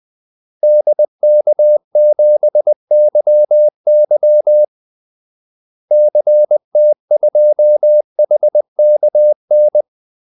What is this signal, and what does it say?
Morse code, heard as a keyed tone, about 20 wpm, 600 Hz. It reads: DK7YY CT2HKN
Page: Kaldesignaler 20 wpm